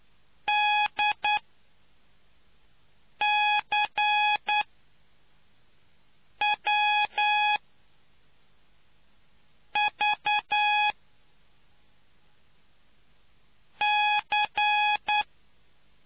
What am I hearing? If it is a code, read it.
Morse code, heard as a keyed tone, about 10 wpm, spaced as for 3 wpm, 840 Hz. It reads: DCWVC